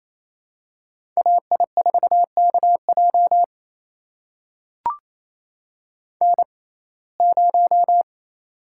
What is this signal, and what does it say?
Morse code, heard as a keyed tone, about 28 wpm, spaced as for 20 wpm, 700 Hz. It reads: AI4KJ N0
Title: USA Call Signs - 5 Characters - 6x Character Spacing - 28wpm